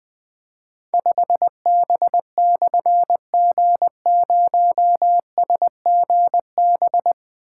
Morse code, heard as a keyed tone, about 20 wpm, 700 Hz. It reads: 5B/G0SGB
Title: Difficult Call Signs 20wpm